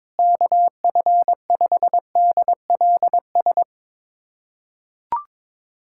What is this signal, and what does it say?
Morse code, heard as a keyed tone, about 22 wpm, 700 Hz. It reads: KF5DLS
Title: USA Callsign DE Callsign - 6 Characters - 22wpm